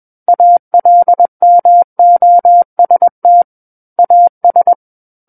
Morse code, heard as a keyed tone, about 21 wpm, 700 Hz. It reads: ALMOST AS
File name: Book One / 12-Chapter 12 - 21 WPM